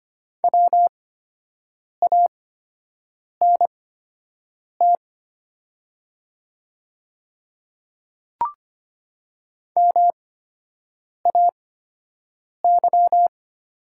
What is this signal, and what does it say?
Morse code, heard as a keyed tone, about 25 wpm, spaced as for 7 wpm, 700 Hz. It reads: WANT MAY
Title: 8x Character Spacing - Top 300 Words - 25wpm